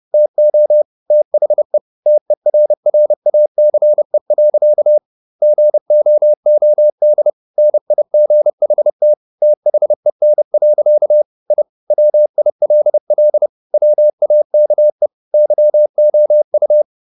Morse code, heard as a keyed tone, about 30 wpm, 600 Hz. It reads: TO THE TERRACE. GOOD NIGHT THEN. I WILL WAKE YOU
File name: the old man and the sea / the old man and the sea 1 30wpm